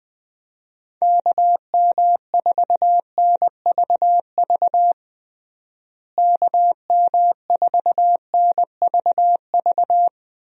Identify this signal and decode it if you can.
Morse code, heard as a keyed tone, about 20 wpm, 700 Hz. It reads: KM4NVV KM4NVV
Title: USA Call Signs - 6 Characters - Repeated 2x - 20wpm